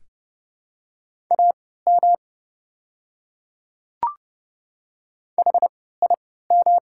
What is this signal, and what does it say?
Morse code, heard as a keyed tone, about 30 wpm, spaced as for 17 wpm, 700 Hz. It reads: AM HIM
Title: 3x Character Spacing - Top 500 Words - 30wpm